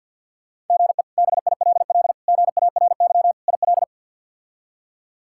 Morse code, heard as a keyed tone, about 52 wpm, 690 Hz. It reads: GE BILL GARY IL
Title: K1USN SST QSO Practice - 50wpm